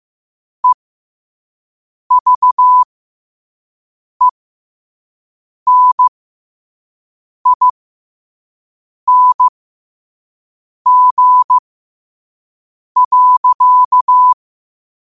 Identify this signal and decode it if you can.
Morse code, heard as a keyed tone, about 15 wpm, spaced as for 5 wpm, 1000 Hz. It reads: EVENING.
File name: qotdfiles / 5wpm